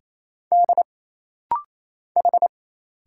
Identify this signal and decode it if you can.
Morse code, heard as a keyed tone, about 27 wpm, 700 Hz. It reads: D H
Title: Single Letters - Rapid-Fire 28wpm